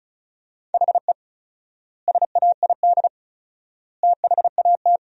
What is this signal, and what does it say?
Morse code, heard as a keyed tone, about 36 wpm, 700 Hz. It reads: HE SAID THAT
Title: 4x Word-Spacing - Sentences from Top 200 Words - 35wpm